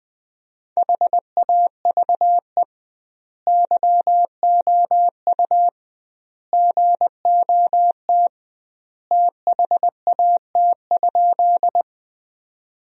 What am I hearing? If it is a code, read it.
Morse code, heard as a keyed tone, about 20 wpm, 700 Hz. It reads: HAVE YOU GOT THAT?
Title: 2x Word-Spacing - Sentences from Top 300 Words - 20wpm